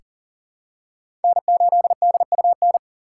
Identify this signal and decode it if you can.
Morse code, heard as a keyed tone, about 40 wpm, 700 Hz. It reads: N8DUN
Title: USA Call Signs - 5 Characters - 4x Character Spacing - 40wpm